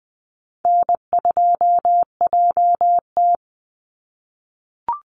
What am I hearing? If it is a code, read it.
Morse code, heard as a keyed tone, about 20 wpm, 700 Hz. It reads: N2JT
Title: Easy Call Signs - 20wpm